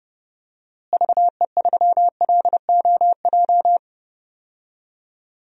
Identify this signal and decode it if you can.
Morse code, heard as a keyed tone, about 30 wpm, 700 Hz. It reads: VE3LOJ
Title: DXCC-100 Call Signs - 6 Characters - Repeated 3x - 30wpm